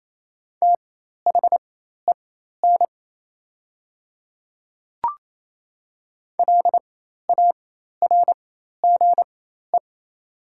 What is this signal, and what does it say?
Morse code, heard as a keyed tone, about 28 wpm, spaced as for 13 wpm, 700 Hz. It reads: THEN LARGE
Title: 4x Character Spacing - Top 200 Words - 28wpm